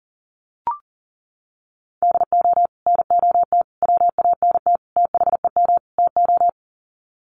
Morse code, heard as a keyed tone, about 38 wpm, 700 Hz. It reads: DO NOT WANT THEM TO
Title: Sets of 5 Words Encore 40wpm